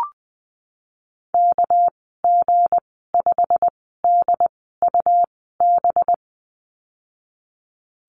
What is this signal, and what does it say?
Morse code, heard as a keyed tone, about 20 wpm, spaced as for 14 wpm, 700 Hz. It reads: KG5DUB
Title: 238 Lesson 38g - Letter q - Call Signs 20wpm